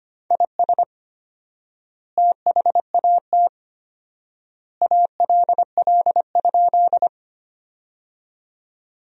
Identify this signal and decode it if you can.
Morse code, heard as a keyed tone, about 25 wpm, 700 Hz. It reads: IS THAT ALL?
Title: 4x Word-Spacing - Sentences from Top 100 Words - 25wpm